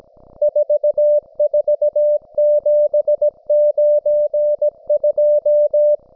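Morse code, heard as a keyed tone, about 17 wpm, 590 Hz. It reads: 44792
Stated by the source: Russian military Morse station, M32, RDL, on 5268 kHz.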